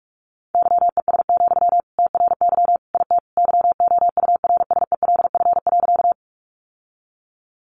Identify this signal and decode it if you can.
Morse code, heard as a keyed tone, about 44 wpm, 700 Hz. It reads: YES, TRY IT YOURSELF.
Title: Sentences from Top 600 Words 45wpm